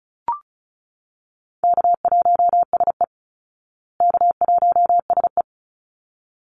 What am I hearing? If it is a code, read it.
Morse code, heard as a keyed tone, about 34 wpm, 700 Hz. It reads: K1SE K1SE
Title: Easy Call Signs Repeated 2x - 35wpm